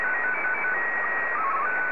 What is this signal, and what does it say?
SSTV (slow-scan television), Martin 1.